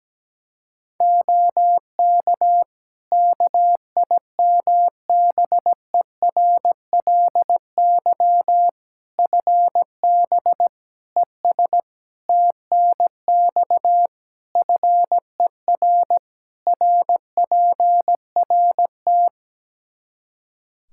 Morse code, heard as a keyed tone, about 17 wpm, 700 Hz. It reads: OK KIMBERLY FB ES TNX FER RPRT
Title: Practice QSO - Exchange 2 - Info - 17wpm